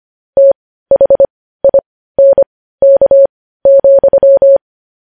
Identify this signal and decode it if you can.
Morse code, heard as a keyed tone, about 25 wpm, spaced as for 15 wpm, 550 Hz. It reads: THINK,
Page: news-headlines-in-morse-code-at-15-wpm